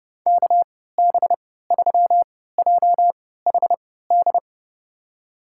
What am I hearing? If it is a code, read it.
Morse code, heard as a keyed tone, about 30 wpm, spaced as for 17 wpm, 700 Hz. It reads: KB3JHD
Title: USA Call Signs - 6 Characters - 3x Character Spacing - 30wpm